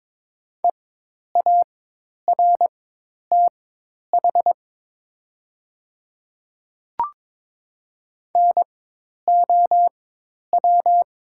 Morse code, heard as a keyed tone, about 22 wpm, spaced as for 10 wpm, 700 Hz. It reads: EARTH NOW
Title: 4x Character Spacing - Top 200 Words - 22wpm